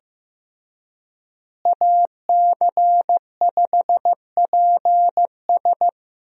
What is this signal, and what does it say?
Morse code, heard as a keyed tone, about 15 wpm, 700 Hz. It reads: AC5PS
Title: USA Call Signs - 5 Characters - Repeated 3x - 15wpm